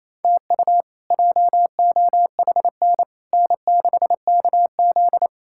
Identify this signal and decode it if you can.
Morse code, heard as a keyed tone, about 28 wpm, 700 Hz. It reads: TU JOHN N6KZ
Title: K1USN SST QSO Practice - 28wpm